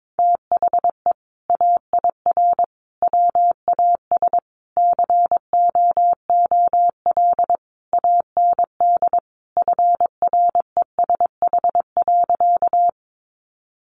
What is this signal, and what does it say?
Morse code, heard as a keyed tone, about 22 wpm, 700 Hz. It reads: THE AIR WAS COOL AND FRESH.
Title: Sentences from Top 1200 Words 22wpm